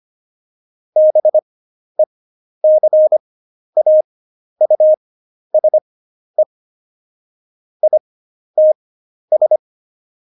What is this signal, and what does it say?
Morse code, heard as a keyed tone, about 25 wpm, spaced as for 11 wpm, 620 Hz. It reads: BECAUSE ITS
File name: Jokes-25-char-10-eff